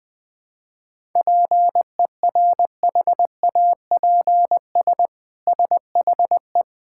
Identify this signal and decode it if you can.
Morse code, heard as a keyed tone, about 20 wpm, 700 Hz. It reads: PERHAPS SHE
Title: Sentences from Top 1000 Words - Review Part 5 - 20wpm